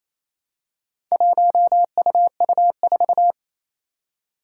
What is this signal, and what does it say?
Morse code, heard as a keyed tone, about 27 wpm, 700 Hz. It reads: 1UU4
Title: Four Challenging Letter-Number Combinations 28wpm